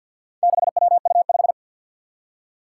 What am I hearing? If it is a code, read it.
Morse code, heard as a keyed tone, about 52 wpm, 700 Hz. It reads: 6WU5
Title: Four Challenging Letter-Number Combinations 50wpm